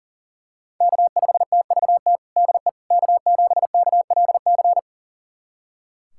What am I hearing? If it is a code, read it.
Morse code, heard as a keyed tone, about 40 wpm, 700 Hz. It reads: K5TVT DE K7KLC
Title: USA Callsign DE Callsign - 5 Characters - 40wpm